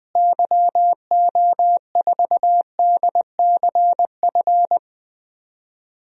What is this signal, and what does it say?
Morse code, heard as a keyed tone, about 20 wpm, 700 Hz. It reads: YO4DCF
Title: DXCC-100 Call Signs - 6 Characters - Repeated 2x - 20wpm